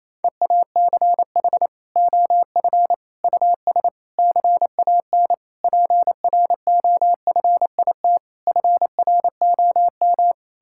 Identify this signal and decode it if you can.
Morse code, heard as a keyed tone, about 27 wpm, 700 Hz. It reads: EACH OF US CAN PROFIT FROM
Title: Sentences from Top 1000 Words - Review Part 5 - 28wpm